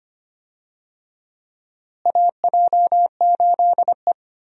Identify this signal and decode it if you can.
Morse code, heard as a keyed tone, about 25 wpm, 700 Hz. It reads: AJ8E
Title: USA Call Signs - 4 Characters - Repeated 2x - 25wpm